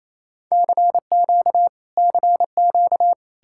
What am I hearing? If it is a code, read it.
Morse code, heard as a keyed tone, about 33 wpm, 700 Hz. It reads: CQ CQ
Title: Speed-Racing Common QSO Phrases - 28wpm